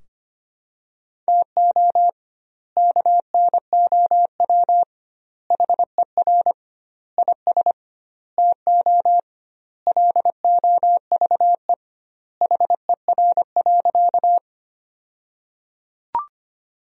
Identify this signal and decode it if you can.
Morse code, heard as a keyed tone, about 25 wpm, 700 Hz. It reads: TO KNOW HER IS TO LOVE HER.
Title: 2x Word-Spacing - Sentences from Top 400 Words - 25wpm